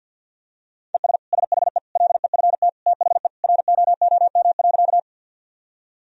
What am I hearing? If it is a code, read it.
Morse code, heard as a keyed tone, about 46 wpm, 700 Hz. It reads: ES SHE LEFT THE ROOM.
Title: Sentences from Top 300 Words and Abbreviations 50wpm